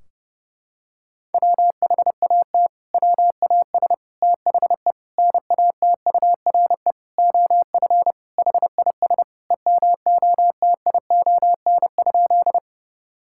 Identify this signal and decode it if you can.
Morse code, heard as a keyed tone, about 30 wpm, 700 Hz. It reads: WHAT WAS THE NATURE OF HIS EMOTION?
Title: Sentences from Top 1600 Words 30wpm